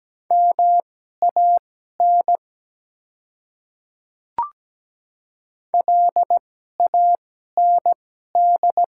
Morse code, heard as a keyed tone, about 17 wpm, spaced as for 12 wpm, 700 Hz. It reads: MAN LAND
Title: 2x Character Spacing - Top 200 Words - 17wpm